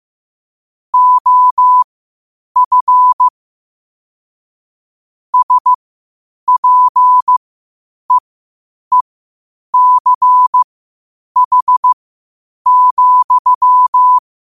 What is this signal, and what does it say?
Morse code, heard as a keyed tone, about 15 wpm, spaced as for 8 wpm, 1000 Hz. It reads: OF SPEECH,
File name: qotdfiles / 8wpm